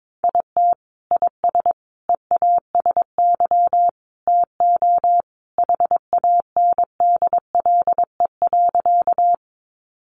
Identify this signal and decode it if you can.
Morse code, heard as a keyed tone, about 22 wpm, 700 Hz. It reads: IT IS EASY TO HANDLE.
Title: Sentences from Top 1100 Words 22wpm